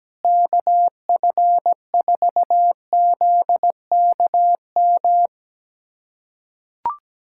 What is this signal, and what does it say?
Morse code, heard as a keyed tone, about 17 wpm, 700 Hz. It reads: KF4ZKM
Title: USA Callsign DE Callsign 17wpm